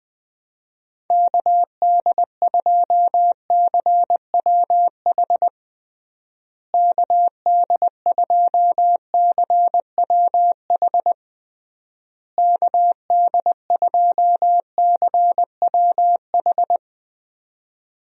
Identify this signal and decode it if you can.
Morse code, heard as a keyed tone, about 20 wpm, 700 Hz. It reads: KD2CWH KD2CWH KD2CWH
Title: USA Call Signs - 6 Characters - Repeated 3x - 20wpm